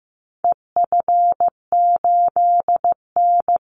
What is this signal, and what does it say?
Morse code, heard as a keyed tone, about 15 wpm, 700 Hz. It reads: EF8N
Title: Call Signs Repeated 3x - 15wpm